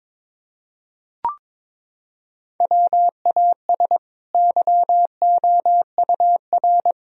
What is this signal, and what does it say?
Morse code, heard as a keyed tone, about 22 wpm, 700 Hz. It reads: WAS YOUR
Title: Sentences from Top 1000 Words - Review Part 4 - 22wpm